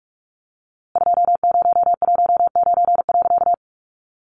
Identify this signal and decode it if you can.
Morse code, heard as a keyed tone, about 45 wpm, 700 Hz. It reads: 2019.
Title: A Sound Of Thunder 45wpm